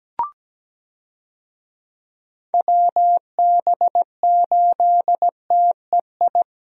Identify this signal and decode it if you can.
Morse code, heard as a keyed tone, about 17 wpm, 700 Hz. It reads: WB8TEI
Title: USA Call Signs - 6 Characters - Repeated 2x - 17wpm